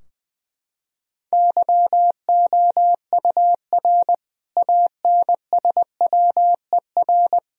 Morse code, heard as a keyed tone, about 20 wpm, 700 Hz. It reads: YOUR ANSWER
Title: Sentences from Top 1000 Words - Review Part 3 - 20wpm